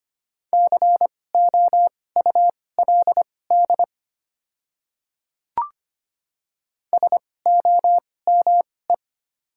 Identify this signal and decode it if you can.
Morse code, heard as a keyed tone, about 25 wpm, spaced as for 18 wpm, 700 Hz. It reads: COULD SOME
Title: 2x Character Spacing - Top 100 Words - 25wpm